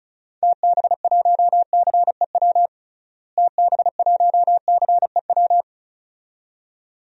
Morse code, heard as a keyed tone, about 34 wpm, 700 Hz. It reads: TB1CEW TB1CEW
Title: DXCC-100 Call Signs - 6 Characters - Repeated 2x - 35wpm